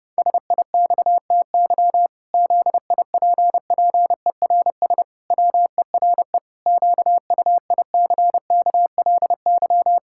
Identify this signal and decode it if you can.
Morse code, heard as a keyed tone, about 30 wpm, 700 Hz. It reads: SIXTY ZIPPERS WERE QUICKLY
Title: Pangram Challenge v2 - 30wpm